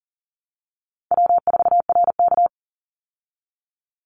Morse code, heard as a keyed tone, about 40 wpm, 700 Hz. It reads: W4RK
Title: Call Signs Repeated 2x - 40wpm